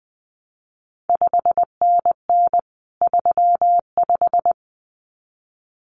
Morse code, heard as a keyed tone, about 19 wpm, 700 Hz. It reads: 5NN 35
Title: CQ WW DX Contest Exchange - 20wpm